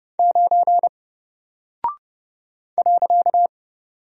Morse code, heard as a keyed tone, about 30 wpm, 700 Hz. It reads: 9 .
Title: Single Character (Letter:Number:Punctuation:Prosign) Alt 30wpm